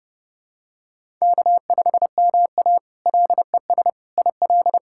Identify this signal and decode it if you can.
Morse code, heard as a keyed tone, about 31 wpm, 700 Hz. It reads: K5MA LES IL
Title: K1USN SST QSO Practice - 30wpm